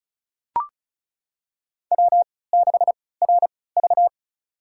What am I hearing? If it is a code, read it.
Morse code, heard as a keyed tone, about 34 wpm, spaced as for 20 wpm, 700 Hz. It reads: W6RV
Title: USA Call Signs - 4 Characters - 3x Character Spacing - 35wpm